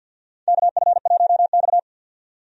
Morse code, heard as a keyed tone, about 52 wpm, 700 Hz. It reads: KF1X
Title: USA Call Signs - 4 Characters - 50wpm